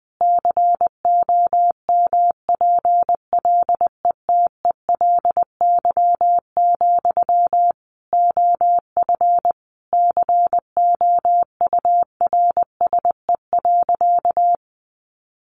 Morse code, heard as a keyed tone, about 20 wpm, 700 Hz. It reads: COMPLETELY, OF COURSE.
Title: Sentences from Top 1100 Words 20wpm